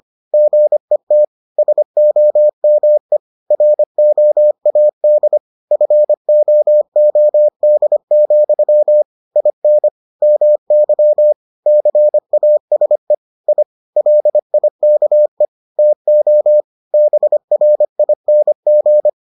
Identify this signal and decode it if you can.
Morse code, heard as a keyed tone, about 25 wpm, 600 Hz. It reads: GET SOME ROAD FOOD, IN MY CASE I LIKE TO BRING